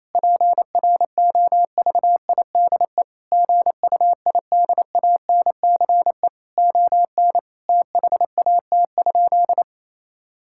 Morse code, heard as a keyed tone, about 28 wpm, 700 Hz. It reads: PROVIDE GUIDANCE ON THAT?
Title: Sentences from Top 1600 Words 28wpm